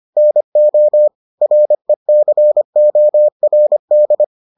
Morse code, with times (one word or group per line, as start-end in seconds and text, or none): 0.2-1.1 NO
1.4-4.2 RECORD